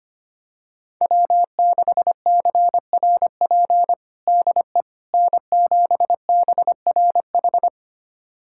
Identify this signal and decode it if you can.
Morse code, heard as a keyed tone, about 25 wpm, 700 Hz. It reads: W6CRP DE N7BRH
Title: USA Callsign DE Callsign - 5 Characters - 25wpm